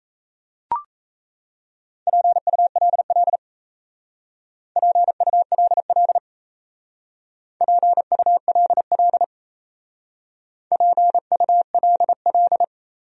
Morse code, heard as a keyed tone, about 35 wpm, 700 Hz. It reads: PULL PULL PULL PULL